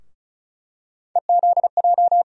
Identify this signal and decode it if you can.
Morse code, heard as a keyed tone, about 36 wpm, 700 Hz. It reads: EZJ
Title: USA Call Sign Postfixes - 3 Letters 35wpm